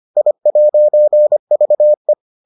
Morse code, heard as a keyed tone, about 25 wpm, 600 Hz. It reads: I'VE